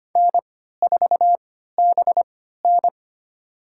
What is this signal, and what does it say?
Morse code, heard as a keyed tone, about 25 wpm, spaced as for 14 wpm, 700 Hz. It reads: N4BN
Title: USA Call Signs - 4 Characters - 3x Character Spacing - 25wpm